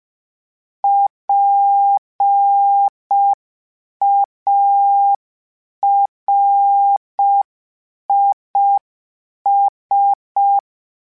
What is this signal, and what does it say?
Morse code, heard as a keyed tone, about 5 wpm, 800 Hz. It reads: PARIS
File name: PARIS Calibration / PARIS mp3s / PARIS@5